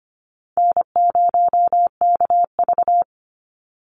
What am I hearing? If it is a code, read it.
Morse code, heard as a keyed tone, about 25 wpm, 700 Hz. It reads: N0KV
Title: Call Signs - 25wpm